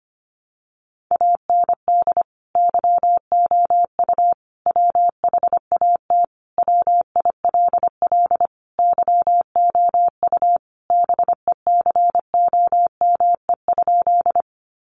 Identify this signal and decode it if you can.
Morse code, heard as a keyed tone, about 25 wpm, 700 Hz. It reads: AND YOU WHAT WILL YOU BECOME?
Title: Sentences from Top 600 Words 25wpm